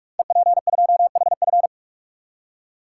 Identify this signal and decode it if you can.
Morse code, heard as a keyed tone, about 44 wpm, 700 Hz. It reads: EP2HF
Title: Code Groups - 5 Letter:Numbers - 45wpm